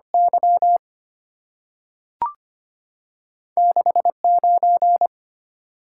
Morse code, heard as a keyed tone, about 25 wpm, 700 Hz. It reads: Y 69
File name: Two Letter-Number Combinations - 25wpm